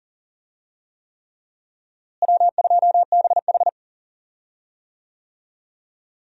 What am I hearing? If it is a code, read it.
Morse code, heard as a keyed tone, about 40 wpm, 700 Hz. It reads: W2BH
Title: USA Call Signs - 4 Characters - Repeated 3x - 40wpm